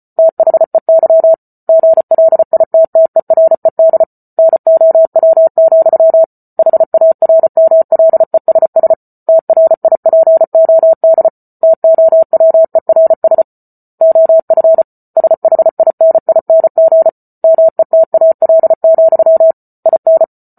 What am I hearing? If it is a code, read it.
Morse code, heard as a keyed tone, about 34 wpm, 650 Hz. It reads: THEY GLITTERED NOW, HARMLESS TRIPOD TOWERS OF SHINING METAL, IN